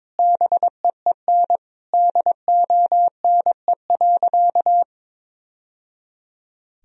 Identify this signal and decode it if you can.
Morse code, heard as a keyed tone, about 22 wpm, 700 Hz. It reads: BEEN DONE.
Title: Sentences from Top 1000 Words - Review Part 1 - 22wpm